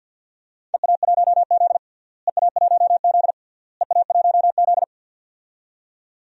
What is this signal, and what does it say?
Morse code, heard as a keyed tone, about 47 wpm, 700 Hz. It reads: EA1Z EA1Z EA1Z
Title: DXCC-100 Call Signs - 4 Characters - Repeated 3x - 50wpm